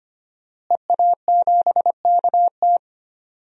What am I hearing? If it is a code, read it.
Morse code, heard as a keyed tone, about 25 wpm, 700 Hz. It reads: EA7KT
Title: DXCC-100 Call Signs - 5 Characters - Repeated 3x - 25wpm